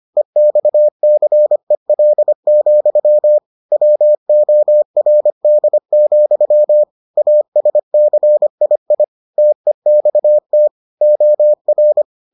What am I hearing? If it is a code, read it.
Morse code, heard as a keyed tone, about 25 wpm, 600 Hz. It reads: EXCEL, WORD, ASCII TEXT OR